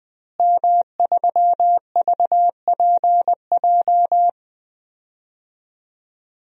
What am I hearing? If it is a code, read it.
Morse code, heard as a keyed tone, about 20 wpm, 700 Hz. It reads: M3VPJ
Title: DXCC-100 Call Signs - 5 Characters - Repeated 2x - 20wpm